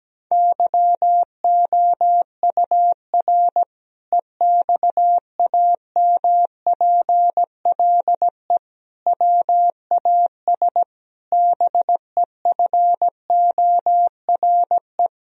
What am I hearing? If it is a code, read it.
Morse code, heard as a keyed tone, about 17 wpm, 700 Hz. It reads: YOUR EXAMPLE WAS BEFORE
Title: Sentences from Top 1000 Words - Review Part 1 - 17wpm